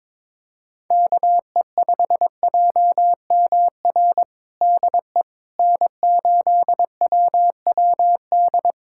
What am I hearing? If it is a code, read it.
Morse code, heard as a keyed tone, about 22 wpm, 700 Hz. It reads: KE5JMR DE N8WWD
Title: USA Callsign DE Callsign 22wpm